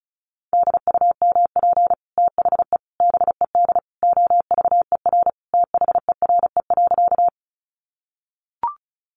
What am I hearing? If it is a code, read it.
Morse code, heard as a keyed tone, about 34 wpm, 700 Hz. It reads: DUMP THE BED OVER THERE.
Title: Sentences from Top 1300 Words 35wpm